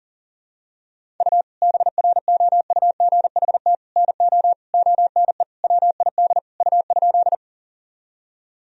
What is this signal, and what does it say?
Morse code, heard as a keyed tone, about 40 wpm, 700 Hz. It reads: U BROUGHT NO ONE WID U?
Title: Sentences from Top 500 Words and Abbreviations 40wpm